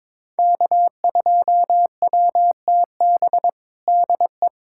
Morse code, heard as a keyed tone, about 22 wpm, 700 Hz. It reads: K2WTB DE
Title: USA Callsign DE Callsign - 5 Characters - 22wpm